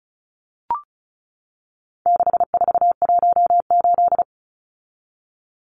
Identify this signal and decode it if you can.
Morse code, heard as a keyed tone, about 34 wpm, 700 Hz. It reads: 6418
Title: Numbers - 4-Digit 35wpm